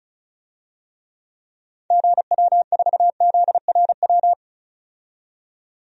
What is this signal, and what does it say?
Morse code, heard as a keyed tone, about 34 wpm, 700 Hz. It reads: GW4ZRW
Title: DXCC-100 Call Signs - 6 Characters - 35wpm